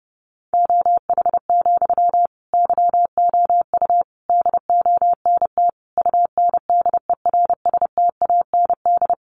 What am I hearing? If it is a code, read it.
Morse code, heard as a keyed tone, about 30 wpm, 700 Hz. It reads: OH, YOU DONT UNDERSTAND
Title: Sentences from Top 1000 Words 30wpm